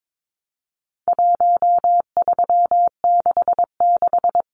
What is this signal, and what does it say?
Morse code, heard as a keyed tone, about 22 wpm, 700 Hz. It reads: 1366
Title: Numbers - Contest Style - 4-Digit 22wpm